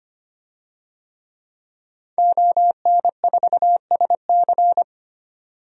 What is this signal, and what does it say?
Morse code, heard as a keyed tone, about 25 wpm, 700 Hz. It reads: ON4SC
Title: DXCC-100 Call Signs - 5 Characters - 25wpm